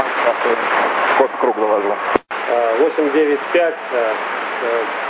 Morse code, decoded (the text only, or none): none